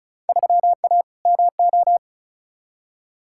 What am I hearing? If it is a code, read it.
Morse code, heard as a keyed tone, about 36 wpm, 700 Hz. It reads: 3A MO
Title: Field Day Exchanges Encore - 35wpm